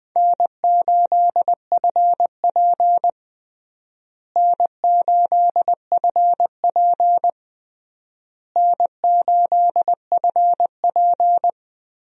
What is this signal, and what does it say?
Morse code, heard as a keyed tone, about 21 wpm, 700 Hz. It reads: N8FP N8FP N8FP
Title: USA Call Signs - 4 Characters - Repeated 3x - 20wpm